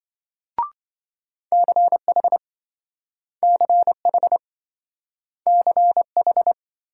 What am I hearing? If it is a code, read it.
Morse code, heard as a keyed tone, about 28 wpm, 700 Hz. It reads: CH CH CH